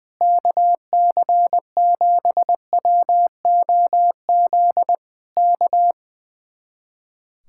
Morse code, heard as a keyed tone, about 20 wpm, 700 Hz. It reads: KC7WOZ K
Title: Speed-Racing Common QSO Phrases - 20wpm